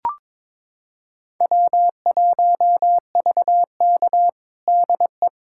Morse code, heard as a keyed tone, about 22 wpm, 700 Hz. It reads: W1VK DE
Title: USA Callsign DE Callsign - 4 Characters - 22wpm